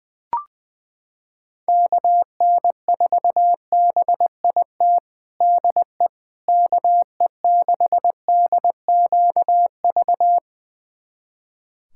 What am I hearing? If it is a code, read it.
Morse code, heard as a keyed tone, about 20 wpm, 700 Hz. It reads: KN4BIT DE KE6DQV
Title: USA Callsign DE Callsign - 6 Characters - 20wpm